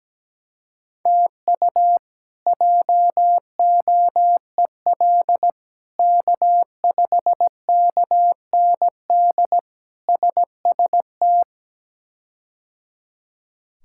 Morse code, heard as a keyed tone, about 17 wpm, 700 Hz. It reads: TU JOEL K5KND SST
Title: K1USN SST QSO Practice - 17wpm